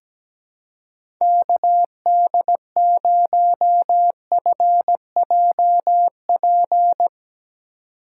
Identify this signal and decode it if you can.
Morse code, heard as a keyed tone, about 17 wpm, 700 Hz. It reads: KD0FJP
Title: USA Call Signs - 6 Characters - 3x Character Spacing - 17wpm